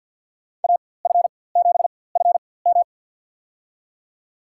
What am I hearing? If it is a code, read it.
Morse code, heard as a keyed tone, about 51 wpm, spaced as for 23 wpm, 700 Hz. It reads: AF7FM
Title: USA Call Signs - 5 Characters - 4x Character Spacing - 50wpm